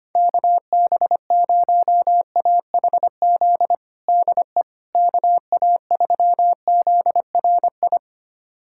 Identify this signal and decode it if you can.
Morse code, heard as a keyed tone, about 25 wpm, 700 Hz. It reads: KB0AHZ DE KA3ZRI
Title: USA Callsign DE Callsign - 6 Characters - 25wpm